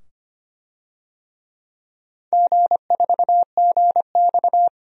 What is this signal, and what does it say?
Morse code, heard as a keyed tone, about 26 wpm, 690 Hz. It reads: G4GX
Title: DXCC-100 Call Signs - 4 Characters - 25wpm